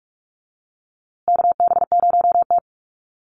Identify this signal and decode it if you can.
Morse code, heard as a keyed tone, about 44 wpm, 700 Hz. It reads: KB0T
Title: USA Call Signs Repeated 3x 45wpm